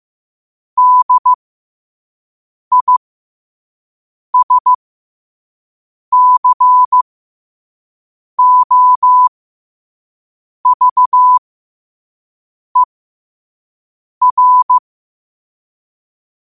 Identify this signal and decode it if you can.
Morse code, heard as a keyed tone, about 15 wpm, spaced as for 5 wpm, 1000 Hz. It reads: DISCOVER
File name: qotdfiles / 5wpm